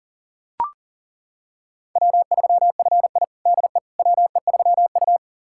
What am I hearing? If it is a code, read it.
Morse code, heard as a keyed tone, about 40 wpm, 700 Hz. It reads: W3FI DE WE3U
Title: USA Callsign DE Callsign - 4 Characters - 40wpm